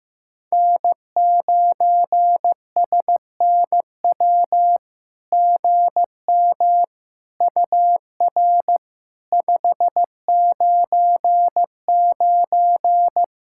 Morse code, heard as a keyed tone, about 15 wpm, 700 Hz. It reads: N9SNW GM UR 599
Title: SOTA QSO Practice 15wpm